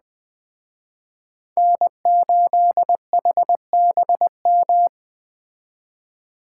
Morse code, heard as a keyed tone, about 20 wpm, 700 Hz. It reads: N8HBM